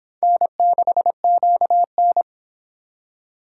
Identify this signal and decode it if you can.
Morse code, heard as a keyed tone, about 26 wpm, 700 Hz. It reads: N6QN